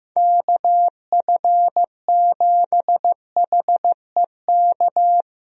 Morse code, heard as a keyed tone, about 15 wpm, 700 Hz. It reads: KF7HEK
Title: USA Call Signs - 6 Characters - Repeated 3x - 15wpm